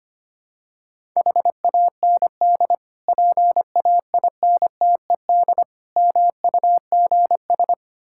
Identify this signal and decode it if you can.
Morse code, heard as a keyed tone, about 25 wpm, 700 Hz. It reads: HAND PAINTED MUGS
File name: practice / 25wpm / Difficult Words with Hints - 25wpm